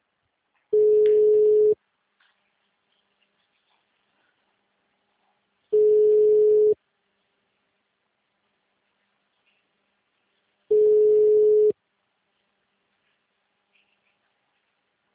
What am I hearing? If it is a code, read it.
Ringback tone (Europe, 425 Hz).